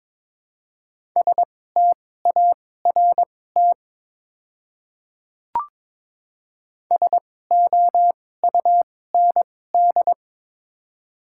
Morse code, heard as a keyed tone, about 23 wpm, spaced as for 16 wpm, 700 Hz. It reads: START SOUND
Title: 2x Character Spacing - Top 500 Words - 22wpm